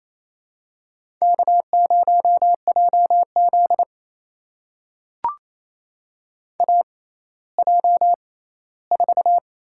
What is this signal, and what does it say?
Morse code, heard as a keyed tone, about 28 wpm, spaced as for 16 wpm, 700 Hz. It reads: K0JZ AJ4
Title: USA Call Signs - 4 Characters - 6x Character Spacing - 28wpm